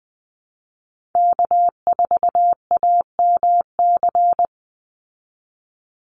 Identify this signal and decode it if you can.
Morse code, heard as a keyed tone, about 20 wpm, 700 Hz. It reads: K4AMC
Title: Call Signs Repeated 2x - 20wpm